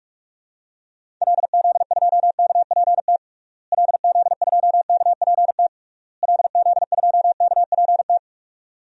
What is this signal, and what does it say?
Morse code, heard as a keyed tone, about 46 wpm, 700 Hz. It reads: LZ2KPT LZ2KPT LZ2KPT
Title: DXCC-100 Call Signs - 6 Characters - Repeated 3x - 45wpm